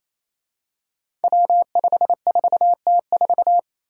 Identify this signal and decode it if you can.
Morse code, heard as a keyed tone, about 28 wpm, 700 Hz. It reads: W54T4
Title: Five Challenging Letter-Number Combinations 28wpm